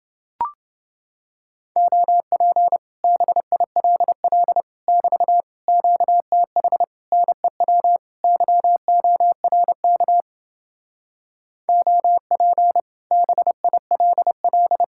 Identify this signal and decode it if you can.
Morse code, heard as a keyed tone, about 30 wpm, 700 Hz. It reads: OP BILL <BT> QTH NEW YORK OP BILL